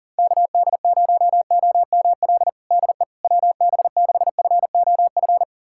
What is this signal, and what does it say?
Morse code, heard as a keyed tone, about 40 wpm, 700 Hz. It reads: KD0OML DE WB6FOF
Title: USA Callsign DE Callsign 40wpm